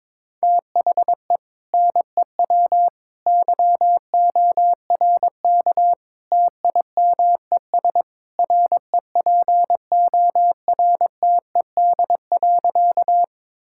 Morse code, heard as a keyed tone, about 22 wpm, 700 Hz. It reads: THE NEW YORK TIMES REPORTED.
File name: practice / 22wpm / Sentences with 3-Word Phrases - 22wpm